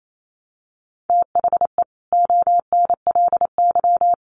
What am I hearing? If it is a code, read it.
Morse code, heard as a keyed tone, about 28 wpm, 700 Hz. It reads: THE ONLY
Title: Sets of 5 Words Encore2 28wpm